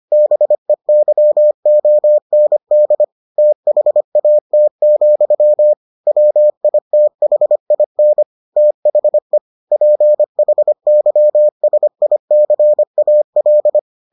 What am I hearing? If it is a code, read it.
Morse code, heard as a keyed tone, about 25 wpm, 600 Hz. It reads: BEYOND THAT, WITHIN THE PHYSICAL